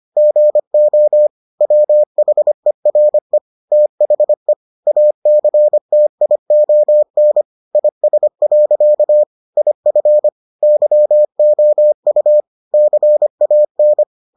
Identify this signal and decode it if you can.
Morse code, heard as a keyed tone, about 25 wpm, 600 Hz. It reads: GO WHERE THE ACTION IS. IF YOU CAN